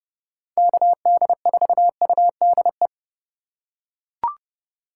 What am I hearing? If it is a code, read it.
Morse code, heard as a keyed tone, about 30 wpm, 700 Hz. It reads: KD4UDE
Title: USA Call Signs - 3x Character Spacing - 30wpm